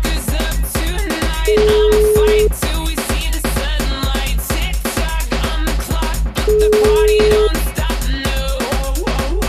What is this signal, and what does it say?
Ringback tone (Europe, 425 Hz), with noise.